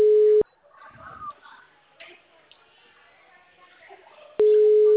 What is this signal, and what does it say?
Ringback tone (Europe, 425 Hz), with speech.